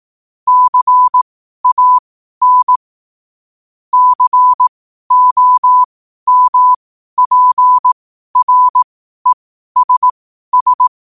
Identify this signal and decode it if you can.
Morse code, heard as a keyed tone, about 18 wpm, spaced as for 12 wpm, 1000 Hz. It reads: CAN COMPRESS
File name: qotdfiles / 12wpm